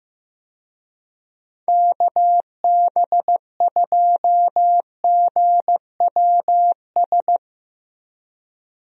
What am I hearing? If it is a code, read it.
Morse code, heard as a keyed tone, about 16 wpm, 700 Hz. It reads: KB2GWS
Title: USA Call Signs - 6 Characters - Repeated 2x - 15wpm